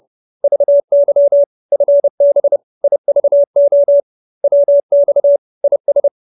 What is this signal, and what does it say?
Morse code, heard as a keyed tone, about 30 wpm, 570 Hz. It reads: VY FB IVO WX IS
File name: QSO319 30